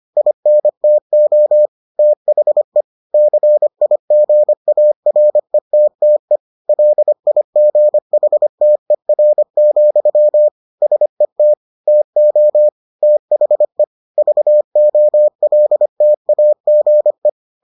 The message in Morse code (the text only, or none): INTO THE CIGARETTE LIGHTER, SET TO THE VOLTAGE